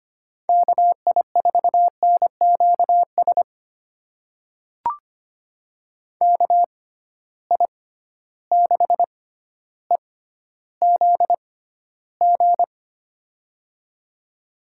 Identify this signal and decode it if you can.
Morse code, heard as a keyed tone, about 25 wpm, spaced as for 13 wpm, 700 Hz. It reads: KI4NQS KI6EZG
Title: USA Call Signs - 6 Characters - 6x Character Spacing - 25wpm